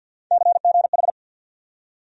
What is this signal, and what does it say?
Morse code, heard as a keyed tone, about 52 wpm, 700 Hz. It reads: CGH
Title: USA Call Sign Postfixes - 3 Letters 50wpm